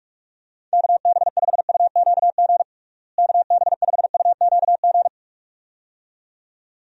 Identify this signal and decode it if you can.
Morse code, heard as a keyed tone, about 41 wpm, 700 Hz. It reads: KB5UQG KB5UQG
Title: USA Call Signs - 6 Characters - Repeated 2x - 45wpm